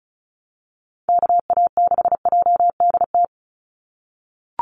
Morse code, heard as a keyed tone, about 34 wpm, 700 Hz. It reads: KA6JDT
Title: USA Call Signs 35wpm